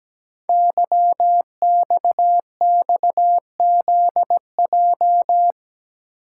Morse code, heard as a keyed tone, about 17 wpm, 700 Hz. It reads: YXXZJ
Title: Code Groups - 5 Letters - 17wpm